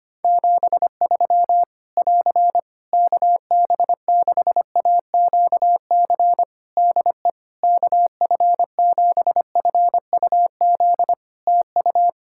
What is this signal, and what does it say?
Morse code, heard as a keyed tone, about 25 wpm, 700 Hz. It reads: 73 <AR> KB6AQC DE KF7FUZ TU
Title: Practice QSO - Exchange 4 - Ending - 25wpm